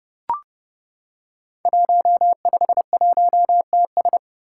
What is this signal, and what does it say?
Morse code, heard as a keyed tone, about 30 wpm, 700 Hz. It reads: 151TS
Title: Five Challenging Letter-Number Combinations 30wpm